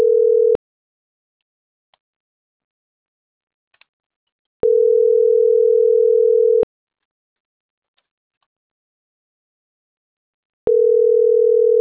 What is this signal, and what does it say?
Ringback tone (North America), 440 + 480 Hz.